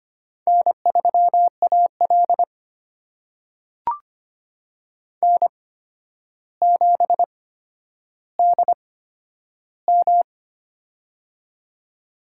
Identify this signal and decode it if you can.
Morse code, heard as a keyed tone, about 25 wpm, spaced as for 11 wpm, 700 Hz. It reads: N3AL N7DM
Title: USA Call Signs - 4 Characters - 8x Character Spacing - 25wpm